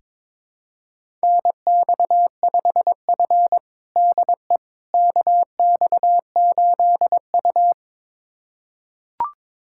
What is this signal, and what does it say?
Morse code, heard as a keyed tone, about 22 wpm, 700 Hz. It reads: NX5F DE KX8U
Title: USA Callsign DE Callsign - 4 Characters - 22wpm